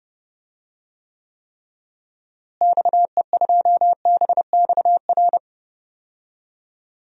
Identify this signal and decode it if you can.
Morse code, heard as a keyed tone, about 30 wpm, 700 Hz. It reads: XE2BXR
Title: DXCC-100 Call Signs - 6 Characters - Repeated 3x - 30wpm